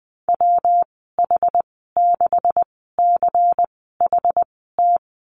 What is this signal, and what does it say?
Morse code, heard as a keyed tone, about 20 wpm, spaced as for 14 wpm, 700 Hz. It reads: WH6CHT
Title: 140 Lesson 23g - Letter f - Call Signs 20wpm